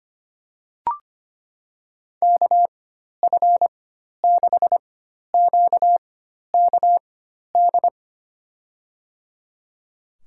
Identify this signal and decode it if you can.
Morse code, heard as a keyed tone, about 25 wpm, spaced as for 12 wpm, 700 Hz. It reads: KF6QKD
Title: USA Call Signs - 6 Characters - 4x Character Spacing - 25wpm